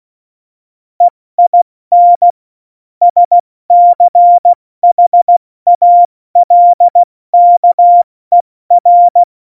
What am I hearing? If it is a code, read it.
Morse code, heard as a keyed tone, about 16 wpm, 700 Hz. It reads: EIN SCHALKER
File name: Morsekurs / Witze / Witz 12. Januar 2021, 70 F80